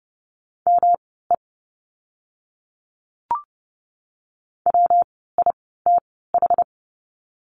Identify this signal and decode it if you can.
Morse code, heard as a keyed tone, about 30 wpm, spaced as for 17 wpm, 700 Hz. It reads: ME WITH
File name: course / 30:10wpm / 237 Lesson 38f - Letter q - Words 30wpm